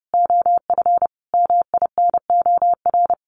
Morse code, heard as a keyed tone, about 30 wpm, 700 Hz. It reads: OF MINOR
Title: Sentences from Top 1500 Words 30wpm